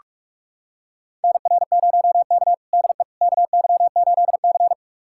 Morse code, heard as a keyed tone, about 41 wpm, 700 Hz. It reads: NR0K DE KY8C
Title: USA Callsign DE Callsign - 4 Characters - 45wpm